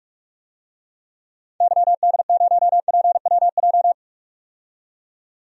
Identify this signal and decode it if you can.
Morse code, heard as a keyed tone, about 42 wpm, 700 Hz. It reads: YD0PWJ
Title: DXCC-100 Call Signs - 6 Characters - 45wpm